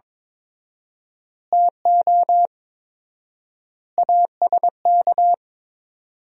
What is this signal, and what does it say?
Morse code, heard as a keyed tone, about 22 wpm, 700 Hz. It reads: TO ASK
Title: 4x Word-Spacing - Sentences from Top 200 Words - 22wpm